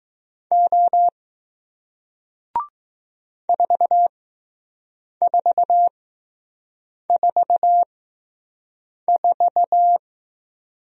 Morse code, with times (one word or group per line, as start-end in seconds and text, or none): 0.5-1.1 O
3.5-10.0 4444